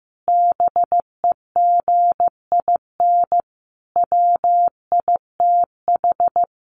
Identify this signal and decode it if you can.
Morse code, heard as a keyed tone, about 15 wpm, 700 Hz. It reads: BEGIN WITH
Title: Sentences from Top 1500 Words 15wpm